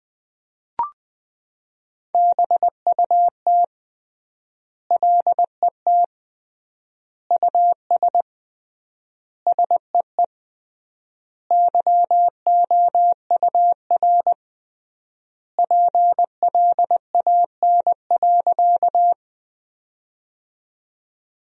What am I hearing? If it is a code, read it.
Morse code, heard as a keyed tone, about 20 wpm, 700 Hz. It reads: BUT LET US SEE YOUR PLAN.
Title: 3x Word-Spacing - Sentences from Top 500 Words - 20wpm